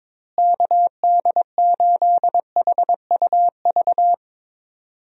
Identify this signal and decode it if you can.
Morse code, heard as a keyed tone, about 22 wpm, 700 Hz. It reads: KD8HUV
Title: USA Call Signs - 6 Characters - Repeated 2x - 22wpm